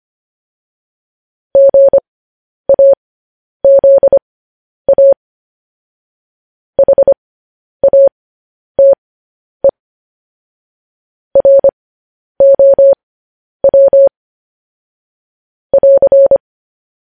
Morse code, heard as a keyed tone, about 25 wpm, spaced as for 10 wpm, 550 Hz. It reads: GAZA HATE ROW <AR>